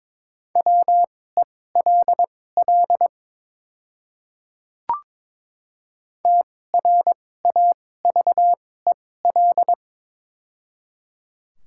Morse code, heard as a keyed tone, about 21 wpm, spaced as for 16 wpm, 700 Hz. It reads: WELL TRAVEL
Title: 2x Character Spacing - Top 500 Words - 22wpm